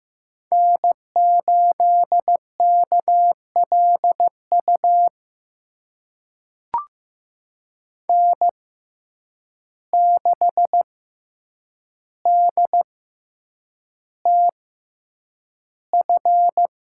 Morse code, heard as a keyed tone, about 15 wpm, spaced as for 8 wpm, 700 Hz. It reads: N8KLU N6DTF
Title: USA Call Signs - 5 Characters - 6x Character Spacing - 15wpm